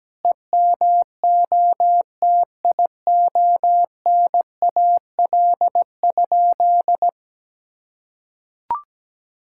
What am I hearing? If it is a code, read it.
Morse code, heard as a keyed tone, about 17 wpm, 700 Hz. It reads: EMOTIONAL?
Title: Sentences from Top 1000 Words - Review Part 5 - 17wpm